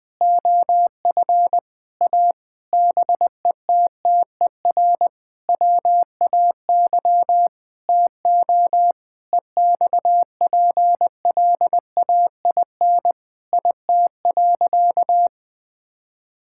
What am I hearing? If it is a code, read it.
Morse code, heard as a keyed tone, about 21 wpm, 700 Hz. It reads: OF A BETTER WAY TO EXPLAIN IT.
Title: Sentences with 8-Word Phrases - 20wpm